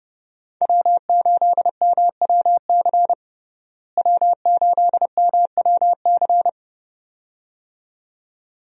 Morse code, heard as a keyed tone, about 30 wpm, 700 Hz. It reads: W8MWC W8MWC
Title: USA Call Signs - 5 Characters - Repeated 2x - 30wpm